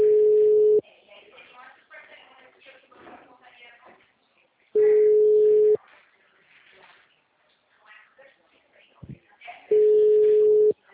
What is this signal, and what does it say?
Ringback tone (Europe, 425 Hz), with speech.
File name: out- -2014- - - .264